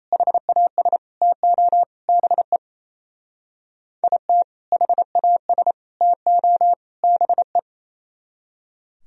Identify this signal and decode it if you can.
Morse code, heard as a keyed tone, about 34 wpm, 700 Hz. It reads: HAS TO BE IT HAS TO BE
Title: Speed-Racing - Sets of 4 Words 28wpm